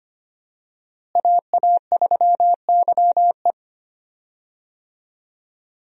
Morse code, heard as a keyed tone, about 25 wpm, 700 Hz. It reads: AA3YE
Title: USA Call Signs - 5 Characters - Repeated 3x - 25wpm